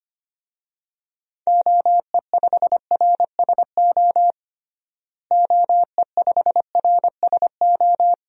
Morse code, heard as a keyed tone, about 25 wpm, 700 Hz. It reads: OE5RSO OE5RSO
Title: DXCC-100 Call Signs - 6 Characters - Repeated 2x - 25wpm